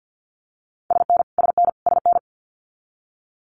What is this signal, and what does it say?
Morse code, heard as a keyed tone, about 50 wpm, 700 Hz. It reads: SN SN SN
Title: Multi-letter Phonemes Repeated 3x 50wpm